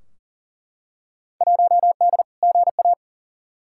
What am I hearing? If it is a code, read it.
Morse code, heard as a keyed tone, about 40 wpm, 700 Hz. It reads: 1D GA
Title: Field Day Exchanges - 40wpm